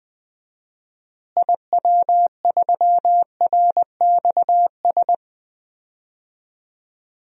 Morse code, heard as a keyed tone, about 20 wpm, 700 Hz. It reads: IW3RXS
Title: DXCC-100 Call Signs - 6 Characters - Repeated 3x - 20wpm